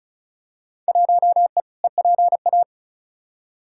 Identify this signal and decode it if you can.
Morse code, heard as a keyed tone, about 36 wpm, 700 Hz. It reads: 1E EPA
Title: Field Day Exchanges - 35wpm